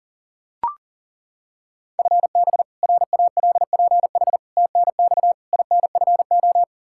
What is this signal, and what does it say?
Morse code, heard as a keyed tone, about 40 wpm, 700 Hz. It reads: FB RALPH TNX INFO
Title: Practice QSO - Exchange 3 - More Info - 40wpm